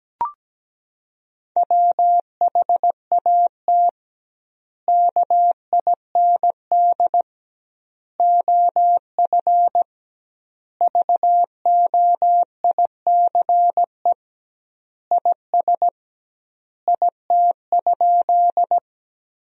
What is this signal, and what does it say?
Morse code, heard as a keyed tone, about 17 wpm, 700 Hz. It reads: WHAT KIND OF VOICE IS IT?
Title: 2x Word-Spacing - Sentences from Top 500 Words - 17wpm